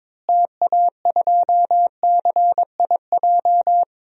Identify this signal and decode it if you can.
Morse code, heard as a keyed tone, about 21 wpm, 700 Hz. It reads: TA2CIJ
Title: DXCC-100 Call Signs - 6 Characters - Repeated 3x - 22wpm